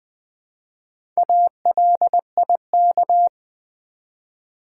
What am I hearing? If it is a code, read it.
Morse code, heard as a keyed tone, about 20 wpm, 700 Hz. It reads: ALIK
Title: Build A Word - 1 Letter At A Time - 5-Letter Words - 20wpm